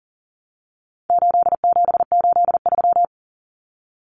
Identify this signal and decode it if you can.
Morse code, heard as a keyed tone, about 40 wpm, 700 Hz. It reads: 8783
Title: Numbers - 4-Digit 40wpm